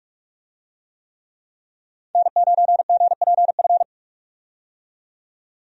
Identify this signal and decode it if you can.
Morse code, heard as a keyed tone, about 42 wpm, 700 Hz. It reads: N9GPF
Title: USA Call Signs - 5 Characters - 45wpm